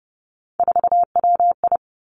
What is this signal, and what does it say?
Morse code, heard as a keyed tone, about 30 wpm, 700 Hz. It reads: 4WI
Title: Three Challenging Letter-Number Combinations 30wpm